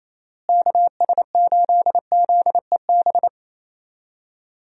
Morse code, heard as a keyed tone, about 28 wpm, 700 Hz. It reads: KS8ZEB
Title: USA Call Signs - 6 Characters - Repeated 3x - 28wpm